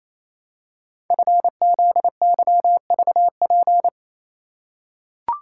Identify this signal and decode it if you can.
Morse code, heard as a keyed tone, about 27 wpm, 700 Hz. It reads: FZYVP
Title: Code Groups - 5 Letter:Numbers - 28wpm